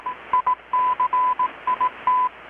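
Morse code, heard as a keyed tone, about 18 wpm, 1000 Hz. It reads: ICIT
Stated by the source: Morse from KPH's Night of Nights XX maritime radiotelegraphy event, received in Chiapas, Mexico about 3,800 km away on a Kenwood R-600 with a grounded 10 m long-wire antenna.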